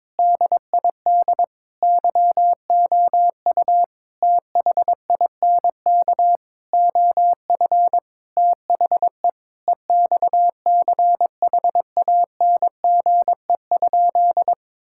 Morse code, heard as a keyed tone, about 23 wpm, 700 Hz. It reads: DID YOU THINK OF THE EXCHANGE?
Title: Sentences from Top 1000 Words - Review Part 6 - 22wpm